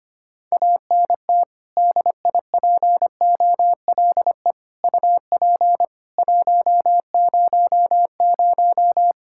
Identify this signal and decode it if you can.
Morse code, heard as a keyed tone, about 25 wpm, 700 Hz. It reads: ANT DIPOLE UP 100
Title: Practice QSO - Exchange 2 - Info - 25wpm